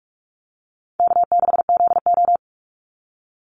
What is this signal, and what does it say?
Morse code, heard as a keyed tone, about 44 wpm, 700 Hz. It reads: K6ZO
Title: Easy Call Signs Repeated 2x - 45wpm